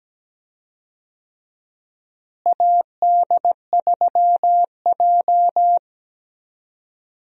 Morse code, heard as a keyed tone, about 17 wpm, 700 Hz. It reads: AD3J
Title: USA Call Signs - 4 Characters - Repeated 3x - 17wpm